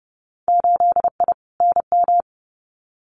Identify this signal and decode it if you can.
Morse code, heard as a keyed tone, about 30 wpm, 700 Hz. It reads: 8I NM
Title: Winter Field Day Exchanges Encore 30wpm